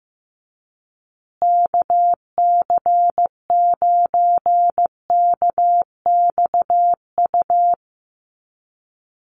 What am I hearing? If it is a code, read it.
Morse code, heard as a keyed tone, about 15 wpm, 700 Hz. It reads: KC9KXU
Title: USA Call Signs Repeated 3x 15wpm